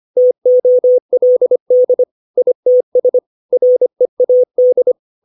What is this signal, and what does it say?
Morse code, heard as a keyed tone, about 25 wpm, 500 Hz. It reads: TOLD ITS READ
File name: 632 words / 632 Words 25WPM